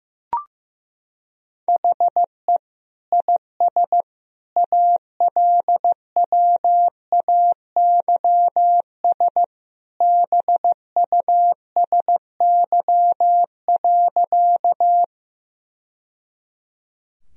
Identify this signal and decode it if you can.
Morse code, heard as a keyed tone, about 16 wpm, 700 Hz. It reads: HE IS ALWAYS BUSY.
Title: Sentences from Top 1000 Words - Review Part 3 - 15wpm